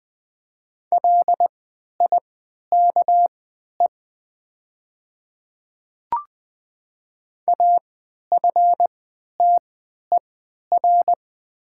Morse code, heard as a keyed tone, about 20 wpm, spaced as for 11 wpm, 700 Hz. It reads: LIKE AFTER
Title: 3x Character Spacing - Top 300 Words - 20wpm